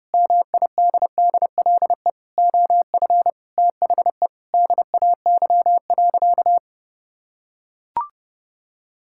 Morse code, heard as a keyed tone, about 30 wpm, 700 Hz. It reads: MIDDLE OF THE DAY.
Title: Sentences from Top 1000 Words - Review Part 6 - 30wpm